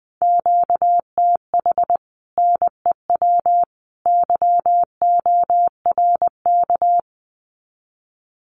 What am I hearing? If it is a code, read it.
Morse code, heard as a keyed tone, about 20 wpm, 700 Hz. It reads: QTH NEW YORK
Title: Common QSO Phrases 20wpm